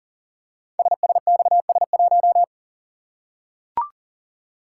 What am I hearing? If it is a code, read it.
Morse code, heard as a keyed tone, about 41 wpm, 690 Hz. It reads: SSXS1
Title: Code Groups - 5 Letter:Numbers - 40wpm